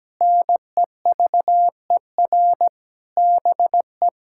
Morse code, heard as a keyed tone, about 17 wpm, 700 Hz. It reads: NEVER BE
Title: Sentences from Top 1600 Words 17wpm